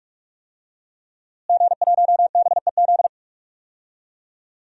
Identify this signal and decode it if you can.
Morse code, heard as a keyed tone, about 46 wpm, 700 Hz. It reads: G1BEZ
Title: DXCC-100 Call Signs - 5 Characters - Repeated 2x - 45wpm